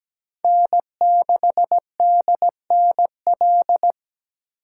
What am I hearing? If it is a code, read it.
Morse code, heard as a keyed tone, about 20 wpm, 700 Hz. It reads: N6DNL
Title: Speed Racing - USA 1x3 Call Signs 17wpm